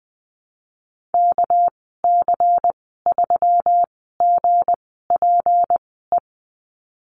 Morse code, heard as a keyed tone, about 20 wpm, spaced as for 14 wpm, 700 Hz. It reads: KC3GPE
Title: 207 Lesson 33g - Letter j - Call Signs 20wpm